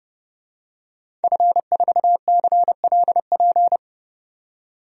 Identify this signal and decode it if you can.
Morse code, heard as a keyed tone, about 30 wpm, 700 Hz. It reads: F4CLP
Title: DXCC-100 Call Signs - 5 Characters - 30wpm